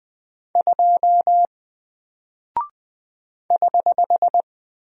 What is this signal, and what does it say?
Morse code, heard as a keyed tone, about 20 wpm, 700 Hz. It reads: 2 <HH>
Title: Single Character (Letter:Number:Punctuation:Prosign) Alt 20wpm